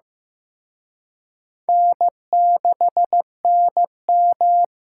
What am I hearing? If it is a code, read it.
Morse code, heard as a keyed tone, about 18 wpm, 700 Hz. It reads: N6NM
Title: Speed Racing - USA 1x2 Call Signs 15wpm